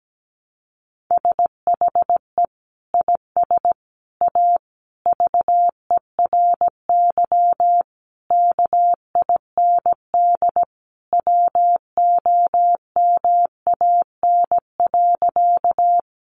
Morse code, heard as a keyed tone, about 17 wpm, 700 Hz. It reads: SHE IS A VERY KIND WOMAN.
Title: Sentences from Top 1100 Words 17wpm